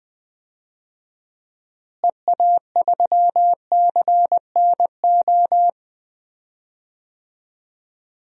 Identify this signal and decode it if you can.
Morse code, heard as a keyed tone, about 20 wpm, 700 Hz. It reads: EA3CNO
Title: DXCC-100 Call Signs - 6 Characters - Repeated 2x - 20wpm